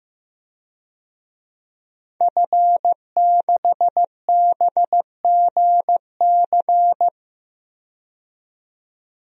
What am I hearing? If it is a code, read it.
Morse code, heard as a keyed tone, about 15 wpm, 700 Hz. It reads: F6BGC
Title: DXCC-100 Call Signs - 5 Characters - 15wpm